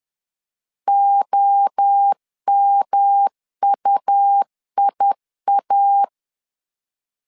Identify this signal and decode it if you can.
Morse code, heard as a keyed tone, about 10 wpm, 800 Hz. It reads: OMUIA
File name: CW / DET / 1 3 1500C S C 10wpm 3335